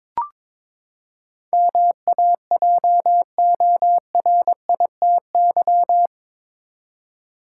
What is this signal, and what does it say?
Morse code, heard as a keyed tone, about 21 wpm, 700 Hz. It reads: MAJORITY
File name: practice / 22wpm / Eight Letter Words - 22wpm